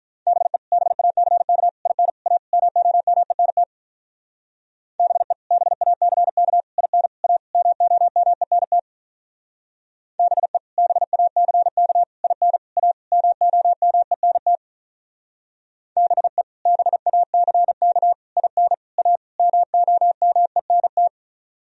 Morse code, heard as a keyed tone, about 43 wpm, 700 Hz. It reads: BE BACK IN A MOMENT BE BACK IN A MOMENT BE BACK IN A MOMENT BE BACK IN A MOMENT
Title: Speed-Racing - Sets of 5 Words 35wpm